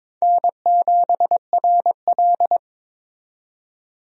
Morse code, heard as a keyed tone, about 22 wpm, 700 Hz. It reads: N7RL
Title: USA Call Signs - 4 Characters - 22wpm